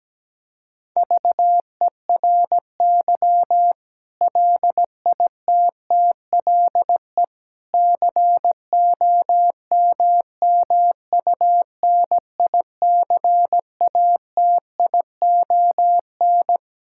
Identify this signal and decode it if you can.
Morse code, heard as a keyed tone, about 17 wpm, 700 Hz. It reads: VERY LITTLE COMMUNICATION
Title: Sentences from Top 1000 Words - Review Part 6 - 17wpm